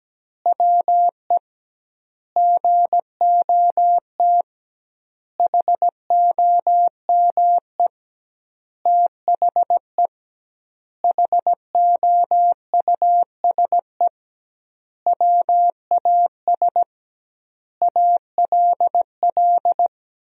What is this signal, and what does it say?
Morse code, heard as a keyed tone, about 17 wpm, 700 Hz. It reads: WE GOT HOME THE HOUSE WAS ALL
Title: 2x Word-Spacing - Sentences from Top 500 Words - 17wpm